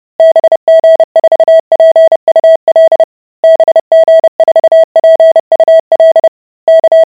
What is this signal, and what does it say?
Morse code, heard as a keyed tone, about 30 wpm, 650 Hz. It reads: BG4PUL BG4PUL K